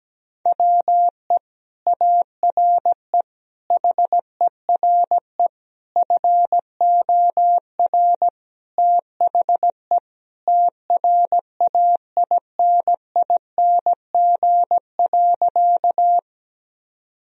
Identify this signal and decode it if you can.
Morse code, heard as a keyed tone, about 17 wpm, 700 Hz. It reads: WE ARE HERE FOR THE TRAINING.
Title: Sentences from Top 1000 Words - Review Part 4 - 17wpm